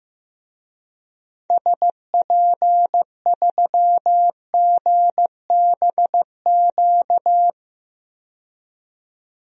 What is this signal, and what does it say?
Morse code, heard as a keyed tone, about 15 wpm, 700 Hz. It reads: SP3GBQ
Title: DXCC-100 Call Signs - 6 Characters - 15wpm